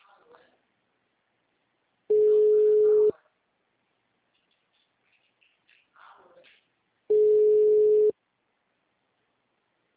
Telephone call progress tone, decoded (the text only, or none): ringback tone